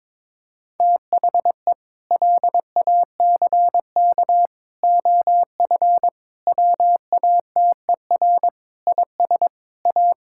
Morse code, heard as a keyed tone, about 21 wpm, 700 Hz. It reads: THE LACK OF WATER IS A
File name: practice / 22wpm / Sentences with 3-Word Phrases - 22wpm